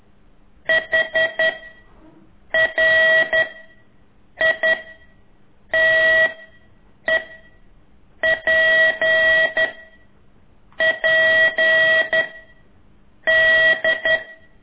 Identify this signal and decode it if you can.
Morse code, heard as a keyed tone, about 11 wpm, 1950 Hz. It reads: H R I T E P P D
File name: ik1weg2 / CORSI CW